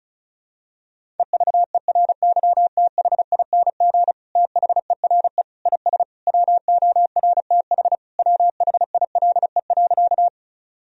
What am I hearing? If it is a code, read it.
Morse code, heard as a keyed tone, about 36 wpm, 700 Hz. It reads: EVERYTHING THERE IS WORTH WHILE.
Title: Sentences from Top 1000 Words - Review Part 1 - 35wpm